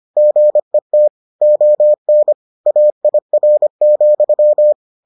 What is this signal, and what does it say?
Morse code, heard as a keyed tone, about 25 wpm, 600 Hz. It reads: GET ON AIR,